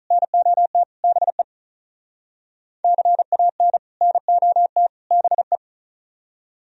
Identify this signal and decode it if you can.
Morse code, heard as a keyed tone, about 43 wpm, 700 Hz. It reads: NOT BE CAN NOT BE
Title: Speed-Racing - Sets of 3 Words 35wpm